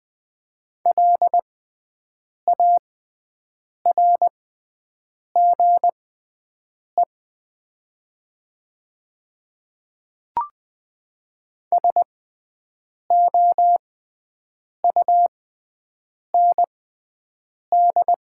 Morse code, heard as a keyed tone, about 20 wpm, spaced as for 7 wpm, 700 Hz. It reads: LARGE SOUND
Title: 6x Character Spacing - Top 200 Words - 20wpm